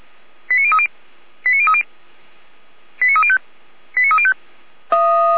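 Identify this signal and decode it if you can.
Five-tone selective call (ZVEI), heard as quick runs of five tones, tones of about 70 ms. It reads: 89030 89030 89305 89305
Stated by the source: Five-tone ZVEI selective call 89140, received by SDR, from feuerwehr-orlamuende fire brigade server.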